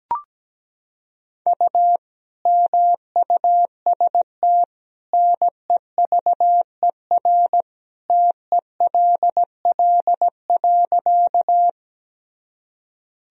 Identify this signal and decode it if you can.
Morse code, heard as a keyed tone, about 17 wpm, 700 Hz. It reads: U MUST NEVER TELL.
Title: Sentences from Top 300 Words and Abbreviations 17wpm